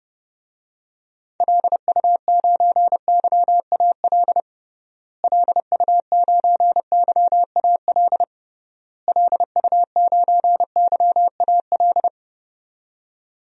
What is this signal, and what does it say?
Morse code, heard as a keyed tone, about 30 wpm, 700 Hz. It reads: LU9YAL LU9YAL LU9YAL
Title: DXCC-100 Call Signs - 6 Characters - Repeated 3x - 30wpm